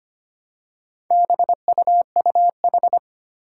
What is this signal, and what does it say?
Morse code, heard as a keyed tone, about 25 wpm, 700 Hz. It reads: BUUH
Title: Four Challenging Letter-Number Combinations 25wpm